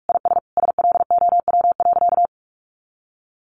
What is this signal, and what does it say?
Morse code, heard as a keyed tone, about 46 wpm, 700 Hz. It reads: IS SLOW.
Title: Sentences from Top 400 Words 45wpm